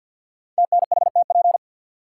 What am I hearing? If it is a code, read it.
Morse code, heard as a keyed tone, about 47 wpm, 700 Hz. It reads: TNHTP
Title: Code Groups - 5 Letters - 50wpm